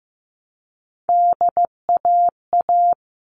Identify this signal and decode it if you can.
Morse code, heard as a keyed tone, about 15 wpm, 700 Hz. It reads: DAA
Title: All Cut Numbers - 3-Digit 15wpm